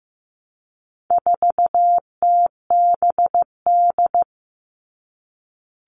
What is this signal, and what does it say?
Morse code, heard as a keyed tone, about 15 wpm, 700 Hz. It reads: 4TBD
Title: All Cut Numbers - 4-Digit 15wpm